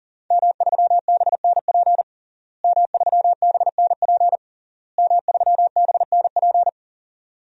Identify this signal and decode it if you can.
Morse code, heard as a keyed tone, about 40 wpm, 700 Hz. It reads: M3BNP M3BNP M3BNP
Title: DXCC-100 Call Signs - 5 Characters - Repeated 3x - 40wpm